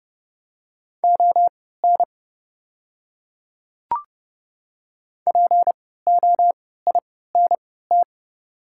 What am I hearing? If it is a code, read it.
Morse code, heard as a keyed tone, about 30 wpm, spaced as for 17 wpm, 700 Hz. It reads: ON POINT
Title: 3x Character Spacing - Top 200 Words - 30wpm